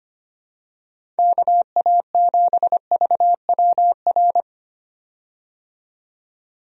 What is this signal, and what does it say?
Morse code, heard as a keyed tone, about 25 wpm, 700 Hz. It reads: KA7VWR
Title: USA Call Signs - 6 Characters - 25wpm